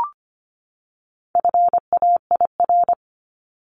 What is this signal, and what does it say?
Morse code, heard as a keyed tone, about 25 wpm, 700 Hz. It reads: FAIR
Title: LY Suffix - Top 100 Words with Base Words 25wpm